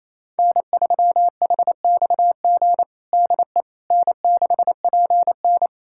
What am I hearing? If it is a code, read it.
Morse code, heard as a keyed tone, about 27 wpm, 700 Hz. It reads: N3HXG DE N6PN
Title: K1USN SST QSO Practice 28wpm